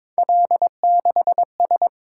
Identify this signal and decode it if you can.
Morse code, heard as a keyed tone, about 22 wpm, 700 Hz. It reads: L6S
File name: practice / 22wpm / Three Letter-Number Combinations - 22wpm